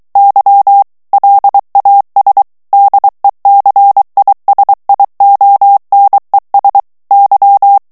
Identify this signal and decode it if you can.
Morse code, heard as a keyed tone, about 23 wpm, 800 Hz. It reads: Y LAS DECISIONES Y
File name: morse mp3 / ESPAGNOL TEXTE EN CLAIR 2 VITESSE 22 MOTS MINUTES